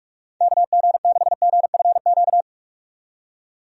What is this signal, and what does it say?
Morse code, heard as a keyed tone, about 46 wpm, 700 Hz. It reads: KG6GFQ
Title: USA Call Signs - 6 Characters - 45wpm